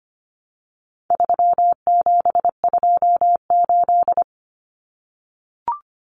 Morse code, heard as a keyed tone, about 25 wpm, 700 Hz. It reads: 3728
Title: Common Cut Numbers - 4-Digit 25wpm